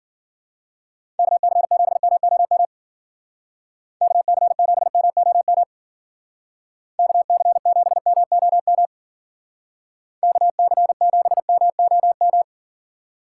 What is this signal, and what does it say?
Morse code, heard as a keyed tone, about 49 wpm, 700 Hz. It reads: KC7MOM KC7MOM KC7MOM KC7MOM
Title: Speed Racing - USA 2x3 Call Signs 40wpm